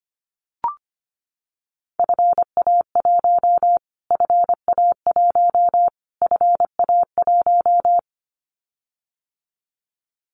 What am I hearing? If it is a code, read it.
Morse code, heard as a keyed tone, about 25 wpm, 700 Hz. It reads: FA1 FA1 FA1
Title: Callsign Prefixes for Top 25 DXCC - Repeated 3x - 25wpm